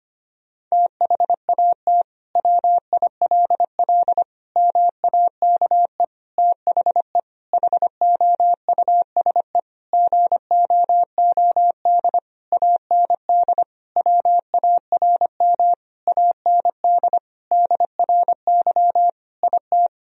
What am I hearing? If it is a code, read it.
Morse code, heard as a keyed tone, about 25 wpm, 700 Hz. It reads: THAT WILL MAKE THE HOUSE GOOD AND WARM AND DRY IT
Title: Sentences from Top 1000 Words - Review Part 3 - 25wpm